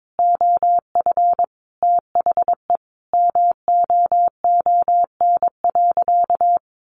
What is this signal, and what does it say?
Morse code, heard as a keyed tone, about 22 wpm, 700 Hz. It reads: OF THE MOON.
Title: Sentences from Top 1500 Words 22wpm